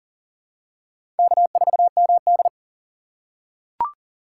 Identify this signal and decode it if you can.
Morse code, heard as a keyed tone, about 38 wpm, 700 Hz. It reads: K4MD
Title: USA Call Signs - 4 Characters - 6x Character Spacing - 40wpm